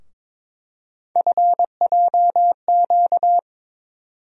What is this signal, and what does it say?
Morse code, heard as a keyed tone, about 22 wpm, 700 Hz. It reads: FJQ
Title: USA Call Sign Postfixes - 3 Letters 22wpm